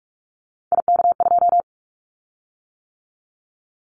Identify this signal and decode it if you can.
Morse code, heard as a keyed tone, about 44 wpm, 700 Hz. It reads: IK2
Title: Callsign Prefixes for Top 50 DXCC - 45wpm